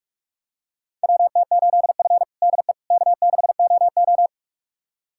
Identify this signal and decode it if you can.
Morse code, heard as a keyed tone, about 46 wpm, 700 Hz. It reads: WT8F DE K6OO
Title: USA Callsign DE Callsign - 4 Characters - 45wpm